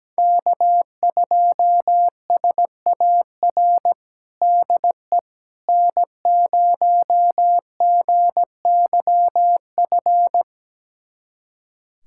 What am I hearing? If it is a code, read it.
Morse code, heard as a keyed tone, about 17 wpm, 700 Hz. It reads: K2SAR DE N0GYF
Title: USA Callsign DE Callsign - 5 Characters - 17wpm